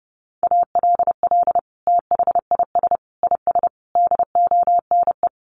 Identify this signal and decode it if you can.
Morse code, heard as a keyed tone, about 30 wpm, 700 Hz. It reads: ALL THIS IS DONE
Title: Sentences from Top 1500 Words 30wpm